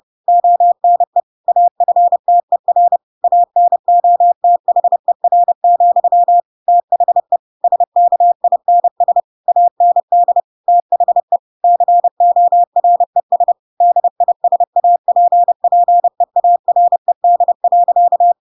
Morse code, heard as a keyed tone, about 30 wpm, 700 Hz. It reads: ONE AFTER ANOTHER, THE SKINS AND THE CORES DISAPPEARED.
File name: Pinocchio / Pinocchio ch7 30 30 ik1hge